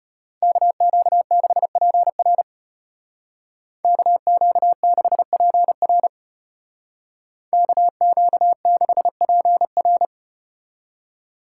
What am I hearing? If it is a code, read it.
Morse code, heard as a keyed tone, about 34 wpm, 700 Hz. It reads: KQ6PR KQ6PR KQ6PR